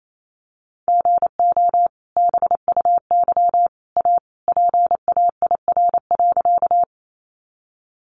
Morse code, heard as a keyed tone, about 28 wpm, 700 Hz. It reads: GO BUY A PAIR.
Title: Sentences from Top 1200 Words 28wpm